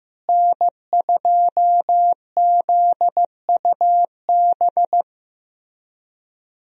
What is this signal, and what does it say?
Morse code, heard as a keyed tone, about 15 wpm, 700 Hz. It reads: N2ZUB
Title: USA Call Signs - 5 Characters - Repeated 3x - 15wpm